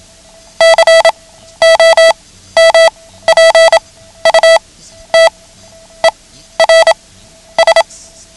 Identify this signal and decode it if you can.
Morse code, heard as a keyed tone, about 27 wpm, spaced as for 12 wpm, 660 Hz. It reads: COMPUTERS
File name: ham radio / CodeQuick / Code Quick 09